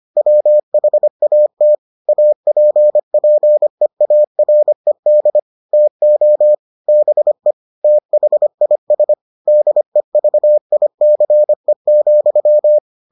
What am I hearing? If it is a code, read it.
Morse code, heard as a keyed tone, about 25 wpm, 600 Hz. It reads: WHAT APPEARED TO BE THIS DEVICE,